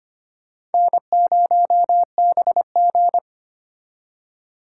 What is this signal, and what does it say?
Morse code, heard as a keyed tone, about 32 wpm, 700 Hz. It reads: N0BG